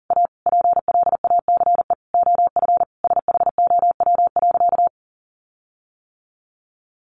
Morse code, heard as a keyed tone, about 40 wpm, 700 Hz. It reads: A PLACE OF SHOW.
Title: Sentences from Top 1400 Words 40wpm